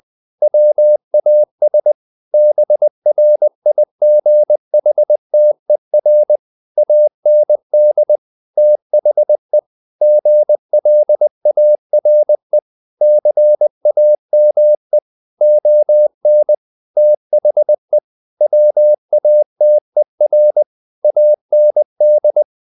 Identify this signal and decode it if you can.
Morse code, heard as a keyed tone, about 20 wpm, 600 Hz. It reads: WAS BRIGHTER AND THE GLARE CAME ON THE WATER AND
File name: the old man and the sea / the old man and the sea part 1 20wpm